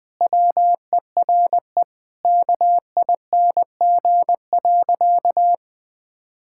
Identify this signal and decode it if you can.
Morse code, heard as a keyed tone, about 20 wpm, 700 Hz. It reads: WERE KING.
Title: Sentences with 4-Word Phrases - 20wpm